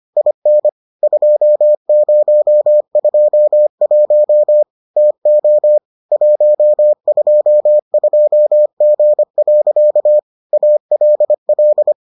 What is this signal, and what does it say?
Morse code, heard as a keyed tone, about 25 wpm, 600 Hz. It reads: IN 2021 TO 122G. ALL